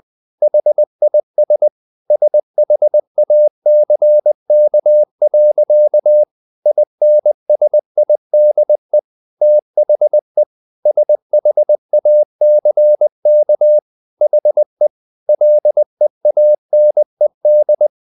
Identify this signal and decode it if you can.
Morse code, heard as a keyed tone, about 20 wpm, 600 Hz. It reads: HIS SHACK. INSIDE THE SHACK HE LEANED
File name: the old man and the sea part 2 20wpm